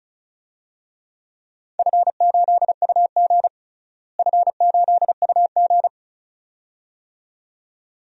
Morse code, heard as a keyed tone, about 36 wpm, 700 Hz. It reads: F8UG F8UG
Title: DXCC-100 Call Signs - 4 Characters - Repeated 2x - 35wpm